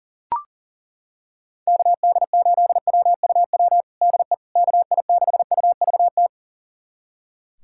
Morse code, heard as a keyed tone, about 40 wpm, 700 Hz. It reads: KD8WUW DE KI6UVT
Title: USA Callsign DE Callsign - 6 Characters - 40wpm